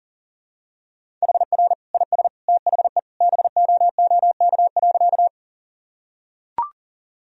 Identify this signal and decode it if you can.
Morse code, heard as a keyed tone, about 40 wpm, 700 Hz. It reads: HR IS THE BOOK.
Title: Sentences from Top 300 Words and Abbreviations 40wpm